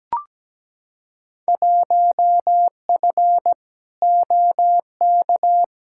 Morse code, heard as a keyed tone, about 17 wpm, 700 Hz. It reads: 1F OK
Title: Field Day Exchanges Encore - 17wpm